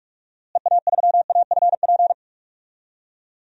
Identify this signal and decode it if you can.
Morse code, heard as a keyed tone, about 46 wpm, 700 Hz. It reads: EA3AFP
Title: DXCC-100 Call Signs - 6 Characters - Repeated 2x - 45wpm